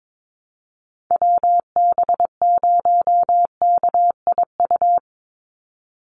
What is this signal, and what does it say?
Morse code, heard as a keyed tone, about 22 wpm, 700 Hz. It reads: WB0KIU
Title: NA SOTA Chaser Callsigns 22wpm